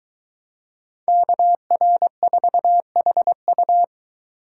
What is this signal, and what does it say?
Morse code, heard as a keyed tone, about 22 wpm, 700 Hz. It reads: KR4HU